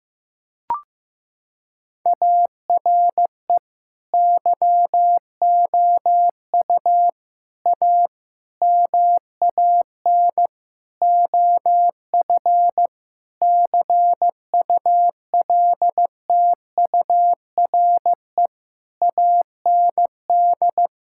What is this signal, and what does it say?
Morse code, heard as a keyed tone, about 16 wpm, 700 Hz. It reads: ARE YOU A MAN OF CULTURE AND
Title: Sentences from Top 1000 Words - Review Part 6 - 15wpm